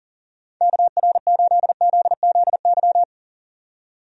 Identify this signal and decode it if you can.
Morse code, heard as a keyed tone, about 40 wpm, 700 Hz. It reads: KR8ZZY
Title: USA Call Signs - 6 Characters - 40wpm